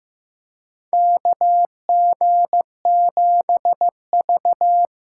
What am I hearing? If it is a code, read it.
Morse code, heard as a keyed tone, about 15 wpm, 700 Hz. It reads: KG7V
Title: USA Call Signs - 6 Characters - 8x Character Spacing - 15wpm